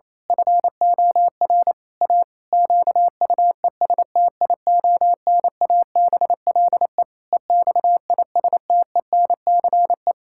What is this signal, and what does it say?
Morse code, heard as a keyed tone, about 28 wpm, 700 Hz. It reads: FOR A QUESTIONABLE EXISTENCE
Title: Pangram Challenge v2 - 28wpm